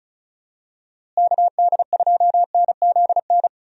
Morse code, heard as a keyed tone, about 36 wpm, 700 Hz. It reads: KD2NZN
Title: USA Call Signs - 6 Characters - Repeated 2x - 35wpm